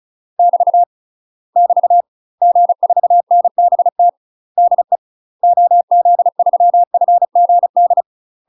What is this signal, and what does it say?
Morse code, heard as a keyed tone, about 34 wpm, 700 Hz. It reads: <BT> <BT> G4NBT DE OZ3FGD
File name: UK MASTER QSOs TRAINING / 35WPM / UK MASTER QSOs 0034